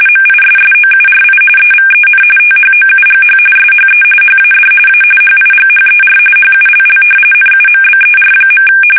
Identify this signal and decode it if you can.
RTTY, about 45.45 baud. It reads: MISSOURI EVEN THOUGH SHE HAD A HIGHER HULL NUMBER.
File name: fsk850 / battleshipnj